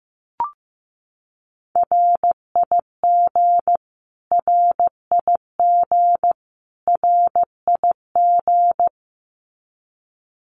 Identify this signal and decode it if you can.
Morse code, heard as a keyed tone, about 15 wpm, 700 Hz. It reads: RIG RIG RIG
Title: Abbreviations Repeated 3x - 15wpm